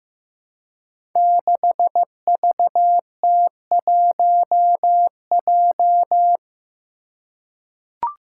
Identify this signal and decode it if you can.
Morse code, heard as a keyed tone, about 15 wpm, 700 Hz. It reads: 6VT1J
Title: Five Challenging Letter-Number Combinations 15wpm